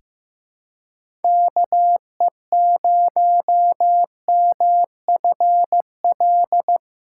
Morse code, heard as a keyed tone, about 15 wpm, 700 Hz. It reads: KE0MFL
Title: USA Callsign DE Callsign - 6 Characters - 15wpm